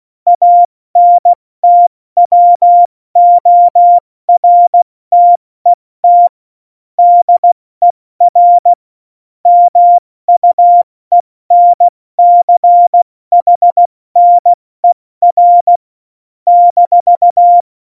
Morse code, heard as a keyed tone, about 16 wpm, spaced as for 14 wpm, 700 Hz. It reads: ANTWORTET DER MUENCHNER -
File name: Morsekurs / Witze / Witz 08. Oktober 2019, 70 F80